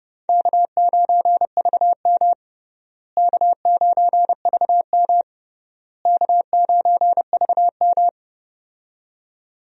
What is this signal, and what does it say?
Morse code, heard as a keyed tone, about 30 wpm, 700 Hz. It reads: K9VM K9VM K9VM
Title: USA Call Signs - 4 Characters - Repeated 3x - 30wpm